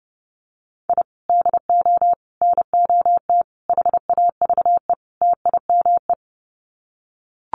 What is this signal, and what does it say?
Morse code, heard as a keyed tone, about 30 wpm, 700 Hz. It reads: I DO NOT HAVE TIME
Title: Sets of 5 Words Encore 30wpm